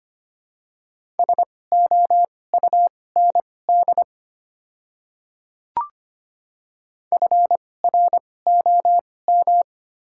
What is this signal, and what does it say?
Morse code, heard as a keyed tone, about 25 wpm, spaced as for 18 wpm, 700 Hz. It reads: SOUND FROM
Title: 2x Character Spacing - Top 100 Words - 25wpm